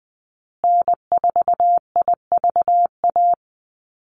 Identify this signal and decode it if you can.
Morse code, heard as a keyed tone, about 20 wpm, 700 Hz. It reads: N4IVA
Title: USA Call Signs Repeated 2x 20wpm